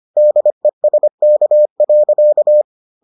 Morse code, heard as a keyed tone, about 25 wpm, 600 Hz. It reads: DESK.